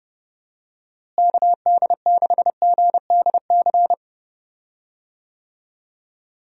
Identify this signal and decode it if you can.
Morse code, heard as a keyed tone, about 30 wpm, 700 Hz. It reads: KD6GDC
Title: USA Call Signs - 6 Characters - Repeated 2x - 30wpm